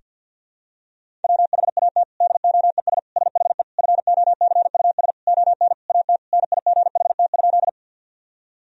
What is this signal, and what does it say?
Morse code, heard as a keyed tone, about 50 wpm, 700 Hz. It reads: WHAT DOES SHE FOCUS ON AT NIGHT?
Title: Sentences from Top 1000 Words - Review Part 4 - 50wpm